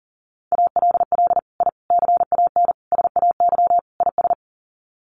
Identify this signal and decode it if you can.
Morse code, heard as a keyed tone, about 40 wpm, 700 Hz. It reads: ALL I CAN SAY IS
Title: Sets of 5 Words Encore2 40wpm